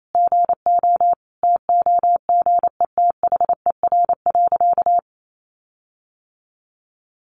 Morse code, heard as a keyed tone, about 28 wpm, 700 Hz. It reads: GO TOGETHER.
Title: Sentences from Top 1500 Words 28wpm